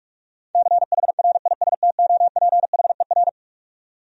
Morse code, heard as a keyed tone, about 48 wpm, 700 Hz. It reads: CHRISTOPHER
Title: USA Common Names - 45wpm